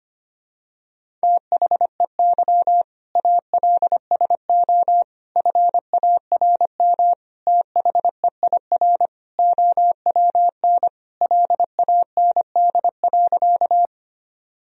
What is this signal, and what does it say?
Morse code, heard as a keyed tone, about 25 wpm, 700 Hz. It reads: THEY ALSO FARM THEIR OWN LAND.
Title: Sentences from Top 1000 Words - Review Part 2 - 25wpm